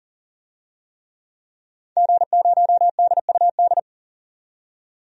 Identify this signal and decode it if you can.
Morse code, heard as a keyed tone, about 40 wpm, 700 Hz. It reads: G0DUD
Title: DXCC-100 Call Signs - 5 Characters - 40wpm